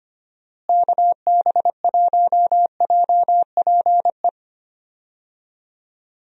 Morse code, heard as a keyed tone, about 25 wpm, 700 Hz. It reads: KB1JPE
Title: USA Call Signs - 6 Characters - Repeated 3x - 25wpm